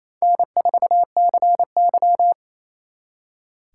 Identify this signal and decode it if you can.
Morse code, heard as a keyed tone, about 36 wpm, 700 Hz. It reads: N4CY